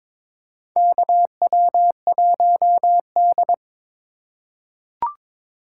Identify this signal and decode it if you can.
Morse code, heard as a keyed tone, about 22 wpm, 700 Hz. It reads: KW1D
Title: USA Call Signs - 4x Character Spacing - 22wpm